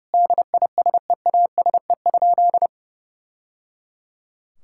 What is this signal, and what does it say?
Morse code, heard as a keyed tone, about 28 wpm, 700 Hz. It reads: DISEASE?
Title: Sentences from Top 1000 Words - Review Part 6 - 30wpm